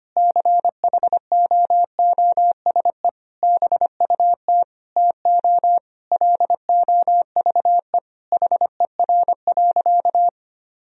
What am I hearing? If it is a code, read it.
Morse code, heard as a keyed tone, about 25 wpm, 700 Hz. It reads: CHOOSE BUT TO LOVE HER.
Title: Sentences from Top 1000 Words - Review Part 4 - 25wpm